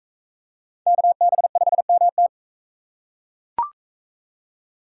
Morse code, heard as a keyed tone, about 39 wpm, 700 Hz. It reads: KB5MT